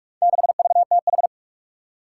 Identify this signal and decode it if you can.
Morse code, heard as a keyed tone, about 44 wpm, 700 Hz. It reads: 6VTH
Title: Four Challenging Letter-Number Combinations 45wpm